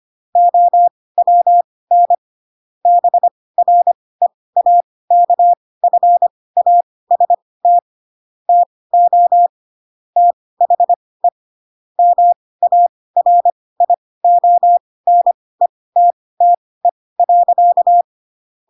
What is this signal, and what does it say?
Morse code, heard as a keyed tone, about 25 wpm, spaced as for 18 wpm, 700 Hz. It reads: OWN BREAKFAST TO THE MARIONETTE.
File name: Pinocchio / Pinocchio ch7 12 25 ik1hge